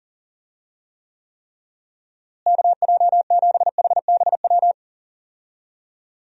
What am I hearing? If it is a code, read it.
Morse code, heard as a keyed tone, about 40 wpm, 700 Hz. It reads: KJ7HBW
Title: USA Call Signs - 6 Characters - Repeated 3x - 40wpm